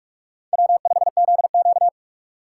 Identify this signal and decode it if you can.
Morse code, heard as a keyed tone, about 48 wpm, 700 Hz. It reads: W5ZQ
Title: USA Call Signs - 4 Characters - 2x Character Spacing - 45wpm